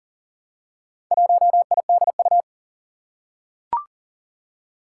Morse code, heard as a keyed tone, about 37 wpm, 700 Hz. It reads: 1IDU
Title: Four Challenging Letter-Number Combinations 40wpm